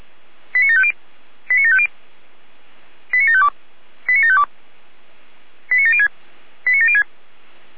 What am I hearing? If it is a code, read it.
Five-tone selective call (ZVEI), heard as quick runs of five tones, tones of about 70 ms. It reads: 89750 89750 89752 89752 89706 89706
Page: SDR/ZVEI